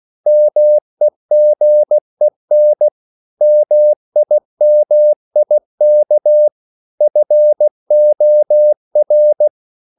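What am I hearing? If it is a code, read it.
Morse code, heard as a keyed tone, about 16 wpm, 600 Hz. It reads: MEGEN MIMIK FOR
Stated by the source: Slow (QRS) Morse code at 16 wpm, from CW med Gnister.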